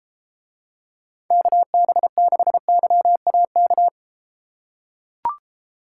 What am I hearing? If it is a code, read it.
Morse code, heard as a keyed tone, about 33 wpm, 700 Hz. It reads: KB6YAK